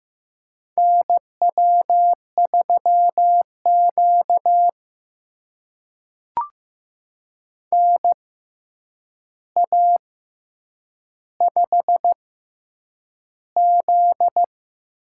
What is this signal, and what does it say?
Morse code, heard as a keyed tone, about 15 wpm, spaced as for 8 wpm, 700 Hz. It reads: NW3Q NA5Z
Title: USA Call Signs - 4 Characters - 6x Character Spacing - 15wpm